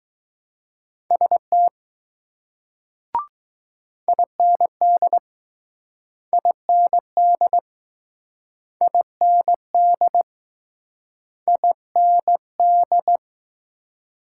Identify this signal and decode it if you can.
Morse code, heard as a keyed tone, about 19 wpm, 700 Hz. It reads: ST IND IND IND IND